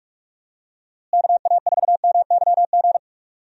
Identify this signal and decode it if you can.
Morse code, heard as a keyed tone, about 46 wpm, 700 Hz. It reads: KA4MYG
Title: USA Call Signs - 6 Characters - Repeated 3x - 45wpm